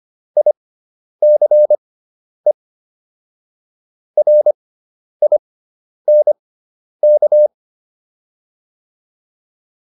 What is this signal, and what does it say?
Morse code, heard as a keyed tone, about 25 wpm, spaced as for 10 wpm, 600 Hz. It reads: ICE RINK